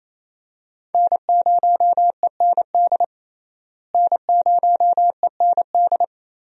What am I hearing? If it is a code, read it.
Morse code, heard as a keyed tone, about 30 wpm, 700 Hz. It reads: N0END N0END
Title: USA Call Signs - 5 Characters - Repeated 3x - 28wpm